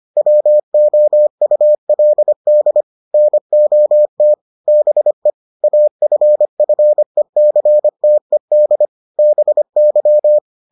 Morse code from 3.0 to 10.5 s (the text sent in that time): NOT BE AFFECTED BY